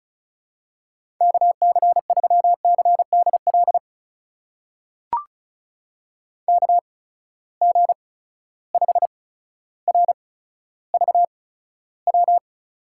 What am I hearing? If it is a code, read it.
Morse code, heard as a keyed tone, about 36 wpm, spaced as for 15 wpm, 700 Hz. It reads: KC3CDL KG5RVW
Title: USA Call Signs - 6 Characters - 8x Character Spacing - 35wpm